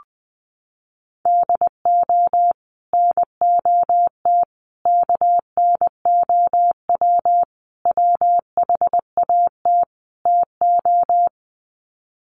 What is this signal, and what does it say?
Morse code, heard as a keyed tone, about 20 wpm, 700 Hz. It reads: DO NOT KNOW WHAT TO
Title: Sets of 5 Words 20wpm